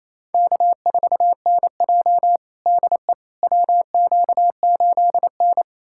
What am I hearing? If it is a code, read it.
Morse code, heard as a keyed tone, about 28 wpm, 700 Hz. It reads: K4NJ DE WQ8N
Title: USA Callsign DE Callsign - 4 Characters - 28wpm